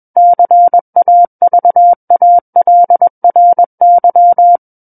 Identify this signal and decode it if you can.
Morse code, heard as a keyed tone, about 21 wpm, 700 Hz. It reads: CAVALRY
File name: Book One / 12-Chapter 12 - 21 WPM